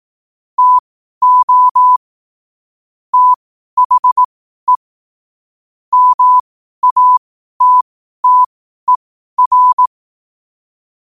Morse code, heard as a keyed tone, about 18 wpm, spaced as for 12 wpm, 1000 Hz. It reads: TO THE MATTER
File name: qotdfiles / 12wpm